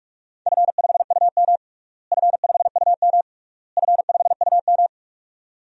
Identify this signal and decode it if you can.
Morse code, heard as a keyed tone, about 46 wpm, 700 Hz. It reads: F5UM F5UM F5UM
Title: DXCC-100 Call Signs - 4 Characters - Repeated 3x - 45wpm